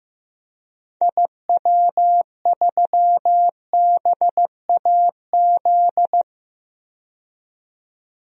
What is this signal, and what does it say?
Morse code, heard as a keyed tone, about 15 wpm, 700 Hz. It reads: IW3BAZ
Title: DXCC-100 Call Signs - 6 Characters - 15wpm